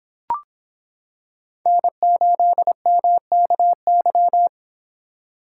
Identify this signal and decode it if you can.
Morse code, heard as a keyed tone, about 25 wpm, 700 Hz. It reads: N8MKY